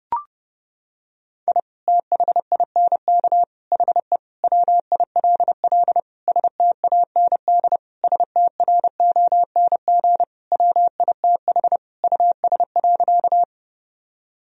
Morse code, heard as a keyed tone, about 30 wpm, 700 Hz. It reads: I THINK HE WILL STAND STRONG WITH US.
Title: Sentences from Top 1000 Words - Review Part 3 - 30wpm